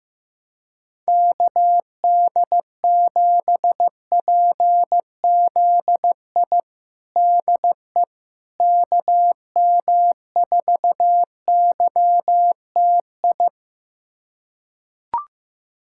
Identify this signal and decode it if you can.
Morse code, heard as a keyed tone, about 15 wpm, 700 Hz. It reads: KD7PZI DE KM4YTI
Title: USA Callsign DE Callsign - 6 Characters - 15wpm